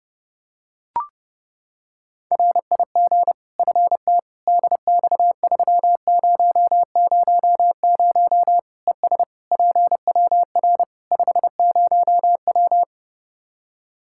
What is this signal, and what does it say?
Morse code, heard as a keyed tone, about 30 wpm, 700 Hz. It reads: RIG FT DX3000 ES PWR 50W
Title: Practice QSO - Exchange 2 - Info - 30wpm